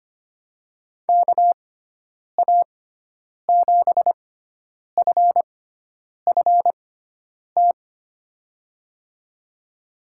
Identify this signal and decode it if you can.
Morse code, heard as a keyed tone, about 25 wpm, spaced as for 9 wpm, 700 Hz. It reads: KA7FFT
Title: USA Call Signs - 6x Character Spacing - 25wpm